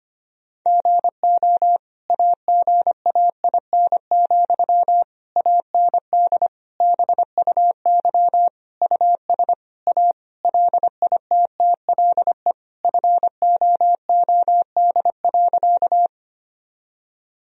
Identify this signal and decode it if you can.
Morse code, heard as a keyed tone, about 25 wpm, 700 Hz. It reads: GO AGAIN, AND BUY US A LITTLE FOOD.
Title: Sentences from Top 1000 Words - Review Part 4 - 25wpm